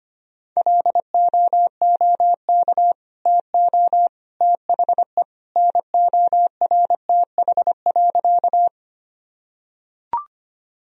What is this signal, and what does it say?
Morse code, heard as a keyed tone, about 25 wpm, 700 Hz. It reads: LOOK TO THE NORTH.
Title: Sentences from Top 1000 Words - Review Part 1 - 25wpm